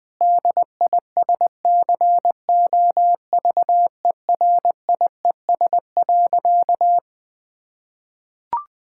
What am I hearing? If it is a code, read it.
Morse code, heard as a keyed tone, about 20 wpm, 700 Hz. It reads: DISCOVERIES.
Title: Difficult Sentences with Hints - 20wpm